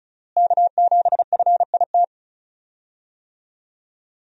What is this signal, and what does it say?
Morse code, heard as a keyed tone, about 36 wpm, 700 Hz. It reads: K7FIT
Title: USA Call Signs - 5 Characters - Repeated 3x - 35wpm